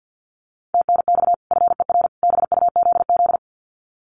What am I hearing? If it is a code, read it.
Morse code, heard as a keyed tone, about 50 wpm, 700 Hz. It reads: TNX FER BUZZ
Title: Common QSO Phrases 50wpm